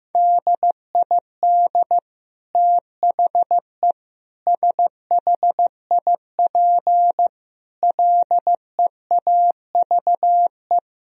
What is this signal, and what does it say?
Morse code, heard as a keyed tone, about 15 wpm, 700 Hz. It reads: DID THE SHIP LEAVE
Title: Sentences from Top 1000 Words - Review Part 2 - 15wpm